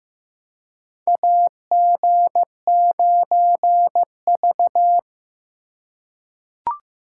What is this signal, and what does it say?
Morse code, heard as a keyed tone, about 15 wpm, 700 Hz. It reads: AG9V
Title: USA Call Signs - 4 Characters - 3x Character Spacing - 15wpm